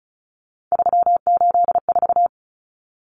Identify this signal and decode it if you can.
Morse code, heard as a keyed tone, about 34 wpm, 700 Hz. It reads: 384
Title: Numbers - 3-Digit 35wpm